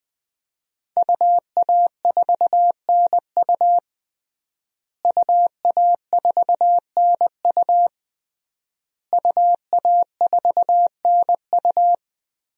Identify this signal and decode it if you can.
Morse code, heard as a keyed tone, about 20 wpm, 700 Hz. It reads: UA4NU UA4NU UA4NU
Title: DXCC-100 Call Signs - 5 Characters - Repeated 3x - 20wpm